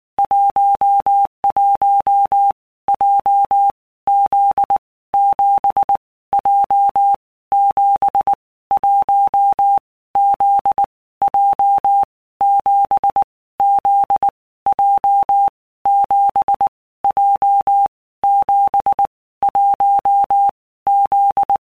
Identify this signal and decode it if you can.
Morse code, heard as a keyed tone, about 19 wpm, 800 Hz. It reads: 11 J Z 7 J 7 1 Z J 7 Z J 7 J 7 1 Z
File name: All Groups 1-minute Blocks / Groups@18 / JZ17